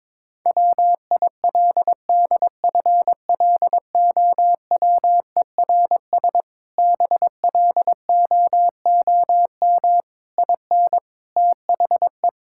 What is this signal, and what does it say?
Morse code, heard as a keyed tone, about 22 wpm, 700 Hz. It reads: WILDFLOWERS BLOOM IN THE
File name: Difficult Sentences with Hints - 22wpm